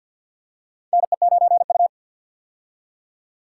Morse code, heard as a keyed tone, about 51 wpm, 700 Hz. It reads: NE9U
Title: K1USN SST QSO Practice - 50wpm